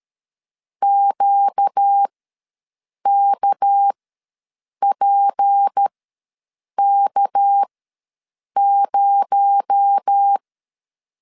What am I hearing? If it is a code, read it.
Morse code, heard as a keyed tone, about 13 wpm, spaced as for 7 wpm, 800 Hz. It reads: QKPK0
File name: LEZIONI / 6 PKX0 / 6 PKX0 ws 12wpm 3448